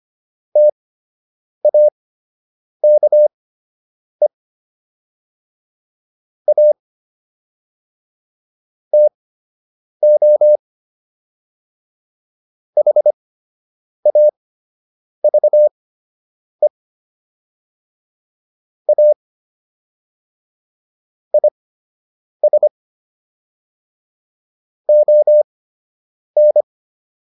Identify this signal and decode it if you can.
Morse code, heard as a keyed tone, about 25 wpm, spaced as for 8 wpm, 600 Hz. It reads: TAKE A TO HAVE A IS ON